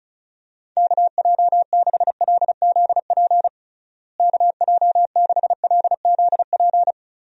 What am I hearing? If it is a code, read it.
Morse code, heard as a keyed tone, about 36 wpm, 700 Hz. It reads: KJ6LZP KJ6LZP
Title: USA Call Signs - 6 Characters - Repeated 2x - 35wpm